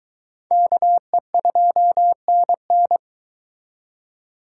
Morse code, heard as a keyed tone, about 22 wpm, 700 Hz. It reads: KE2NN